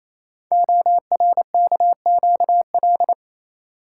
Morse code, heard as a keyed tone, about 27 wpm, 700 Hz. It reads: ORKQL
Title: Code Groups - 5 Letters - 28wpm